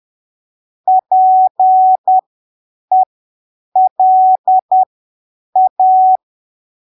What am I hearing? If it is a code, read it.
Morse code, heard as a keyed tone, about 10 wpm, spaced as for 7 wpm, 750 Hz. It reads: PELA
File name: arquivos / texto / texto 14 10ppm